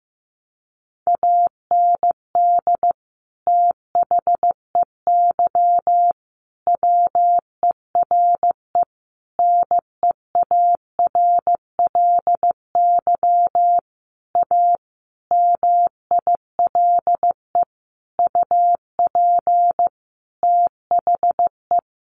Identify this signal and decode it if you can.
Morse code, heard as a keyed tone, about 15 wpm, 700 Hz. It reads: AND THEY WERE NEARLY A MILE UP THE
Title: Sentences from Top 1300 Words 15wpm